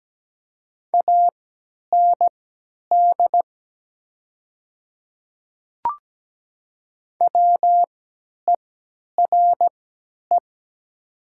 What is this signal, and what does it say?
Morse code, heard as a keyed tone, about 17 wpm, spaced as for 10 wpm, 700 Hz. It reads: AND WERE
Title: 3x Character Spacing - Top 100 Words - 17wpm